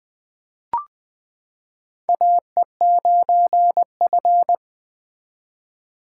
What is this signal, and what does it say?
Morse code, heard as a keyed tone, about 20 wpm, 700 Hz. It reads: AE9F
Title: 2023 SOTA Chaser Call Signs - 20wpm